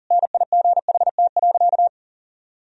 Morse code, heard as a keyed tone, about 40 wpm, 700 Hz. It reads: NIGHT.
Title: Sentences from Top 1000 Words - Review Part 2 - 40wpm